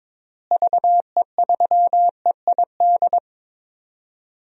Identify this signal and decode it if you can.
Morse code, heard as a keyed tone, about 22 wpm, 700 Hz. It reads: VE3EID
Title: DXCC-100 Call Signs - 6 Characters - Repeated 3x - 22wpm